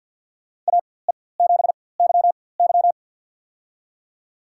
Morse code, heard as a keyed tone, about 52 wpm, spaced as for 24 wpm, 700 Hz. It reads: AE7YY
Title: USA Call Signs - 5 Characters - 4x Character Spacing - 50wpm